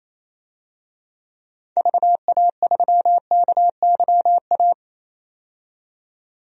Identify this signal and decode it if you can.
Morse code, heard as a keyed tone, about 28 wpm, 700 Hz. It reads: VA3KYA
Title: DXCC-100 Call Signs - 6 Characters - 28wpm